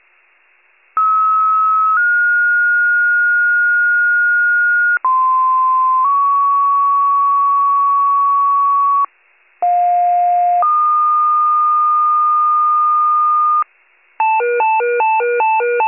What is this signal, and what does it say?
Two-tone page, heard as a one-second tone, then a three-second tone, about 1290 then 1430 Hz, about 1040 then 1120 Hz, about 710 then 1220 Hz. It is heard from 1.0 to 13.6 s.
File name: pager / 95 RESCUE / wcalert